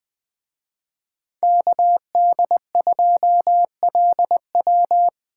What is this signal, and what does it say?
Morse code, heard as a keyed tone, about 20 wpm, 700 Hz. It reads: KD2LW
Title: Speed Racing - USA Call Signs 17wpm